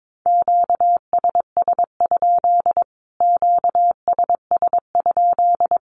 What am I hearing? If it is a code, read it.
Morse code, heard as a keyed tone, about 22 wpm, 700 Hz. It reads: QSS? QSS?
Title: Q-Codes Repeated 3x 22wpm